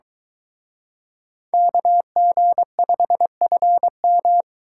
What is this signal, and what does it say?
Morse code, heard as a keyed tone, about 23 wpm, 700 Hz. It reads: KG5FM